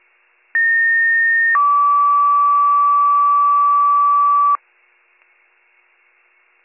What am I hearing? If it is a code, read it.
Two-tone page, heard as a one-second tone, then a three-second tone, about 1850 then 1150 Hz. It is heard from 0.5 to 4.6 s.